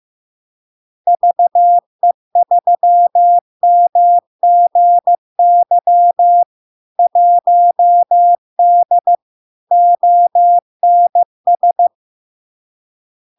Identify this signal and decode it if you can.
Morse code, heard as a keyed tone, about 15 wpm, 700 Hz. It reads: VE3MGY 1D ONS
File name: LARK-FD-2022-Log-in-CW-15wpm-700-Hz